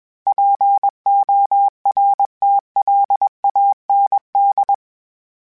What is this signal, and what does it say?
Morse code, heard as a keyed tone, about 21 wpm, 800 Hz. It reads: PORTLAND
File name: fromSMs / FromSMs mp3s / Cities@20